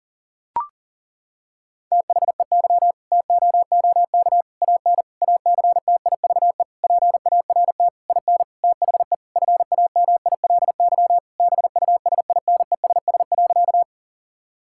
Morse code, heard as a keyed tone, about 40 wpm, 700 Hz. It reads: THEY TOOK AN ACTIVE PART IN THE FAMILY BUSINESS.
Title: Sentences from Top 1000 Words - Review Part 6 - 40wpm